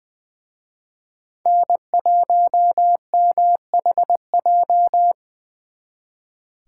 Morse code, heard as a keyed tone, about 23 wpm, 700 Hz. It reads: N1MHJ
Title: Speed Racing - USA Call Signs 20wpm